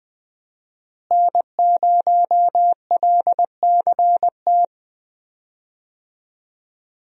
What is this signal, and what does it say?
Morse code, heard as a keyed tone, about 20 wpm, 700 Hz. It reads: N0LCT
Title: USA Call Signs - 5 Characters - Repeated 2x - 20wpm